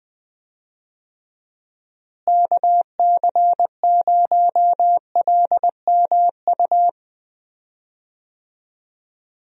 Morse code, heard as a keyed tone, about 20 wpm, 700 Hz. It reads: KC0LMU
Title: USA Call Signs - 6 Characters - Repeated 3x - 20wpm